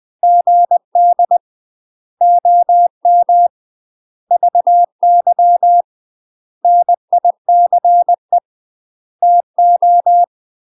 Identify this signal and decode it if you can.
Morse code, heard as a keyed tone, about 20 wpm, 700 Hz. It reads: GD OM VY NICE TO
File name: UK MASTER QSOs TRAINING / 20WPM / UK MASTER QSOs 0051